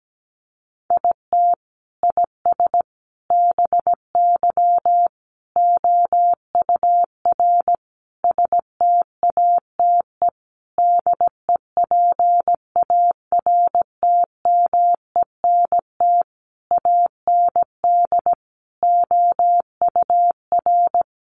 Morse code, heard as a keyed tone, about 17 wpm, 700 Hz. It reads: IT IS BY OUR STATE DEPARTMENT AND OUR
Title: Sentences from Top 1400 Words 17wpm